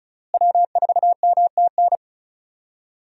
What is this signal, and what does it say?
Morse code, heard as a keyed tone, about 34 wpm, 700 Hz. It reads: W4MTN
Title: USA Call Signs - 5 Characters - Repeated 2x - 35wpm